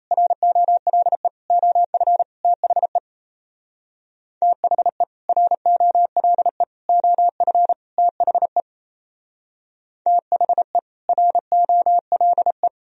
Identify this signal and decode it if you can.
Morse code, heard as a keyed tone, about 36 wpm, 700 Hz. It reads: ROLE OF THE THE ROLE OF THE THE ROLE